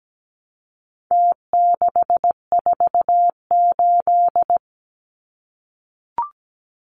Morse code, heard as a keyed tone, about 17 wpm, 700 Hz. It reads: T648
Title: Numbers - Contest Style - 4-Digit 17wpm